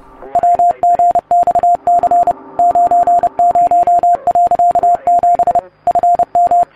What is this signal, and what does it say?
Morse code, heard as a keyed tone, about 30 wpm, 700 Hz. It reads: WGXC 90.7 FM